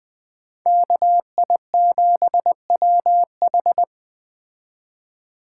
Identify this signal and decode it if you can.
Morse code, heard as a keyed tone, about 21 wpm, 700 Hz. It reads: KI7WH